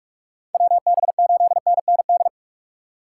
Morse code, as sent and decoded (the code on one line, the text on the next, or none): .-- -... ---.. -. -. -..
WB8NND